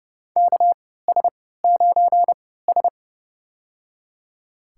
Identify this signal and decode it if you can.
Morse code, heard as a keyed tone, about 30 wpm, spaced as for 17 wpm, 700 Hz. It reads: KS9S
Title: USA Call Signs - 4 Characters - 3x Character Spacing - 30wpm